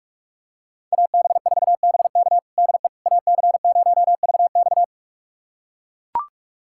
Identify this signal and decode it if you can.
Morse code, heard as a keyed tone, about 42 wpm, 700 Hz. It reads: AB4BK DE AC0VX
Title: USA Callsign DE Callsign - 5 Characters - 45wpm